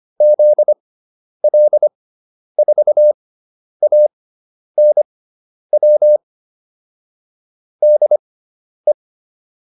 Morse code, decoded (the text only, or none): ZL4ANW DE